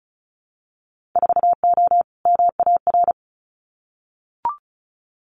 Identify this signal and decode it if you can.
Morse code, heard as a keyed tone, about 34 wpm, 700 Hz. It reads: 4O MAR
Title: Winter Field Day Exchanges Encore 35wpm